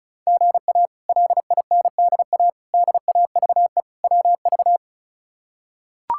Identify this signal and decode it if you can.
Morse code, heard as a keyed tone, about 36 wpm, 700 Hz. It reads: GA LINDA DAVE WV
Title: K1USN SST QSO Practice - 35wpm